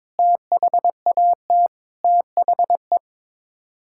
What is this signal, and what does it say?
Morse code, heard as a keyed tone, about 22 wpm, 700 Hz. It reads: THAT THE
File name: practice / 22wpm / Sets of 8 Words - 22wpm